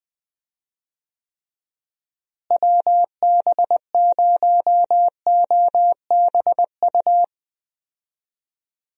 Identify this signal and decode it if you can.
Morse code, heard as a keyed tone, about 20 wpm, 700 Hz. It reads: WB0OBU
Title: USA Call Signs - 6 Characters - Repeated 3x - 20wpm